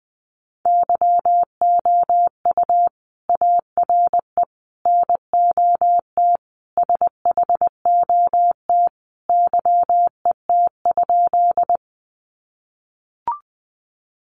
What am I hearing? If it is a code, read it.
Morse code, heard as a keyed tone, about 19 wpm, 700 Hz. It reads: YOU ARE NOT SHOT YET?
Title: Sentences from Top 1100 Words 20wpm